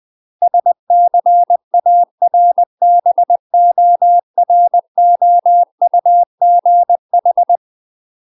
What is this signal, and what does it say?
Morse code, heard as a keyed tone, about 20 wpm, 700 Hz. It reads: SCARBOROUGH
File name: UK MASTER QSOs TRAINING / 20WPM / UK MASTER QSOs 0036